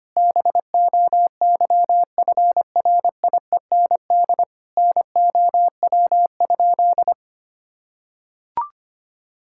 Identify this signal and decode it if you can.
Morse code, heard as a keyed tone, about 25 wpm, 700 Hz. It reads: BOYFRIEND NOW?
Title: Sentences from Top 1000 Words - Review Part 5 - 25wpm